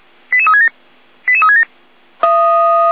Five-tone selective call (ZVEI), heard as quick runs of five tones, tones of about 70 ms. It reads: 88367 88367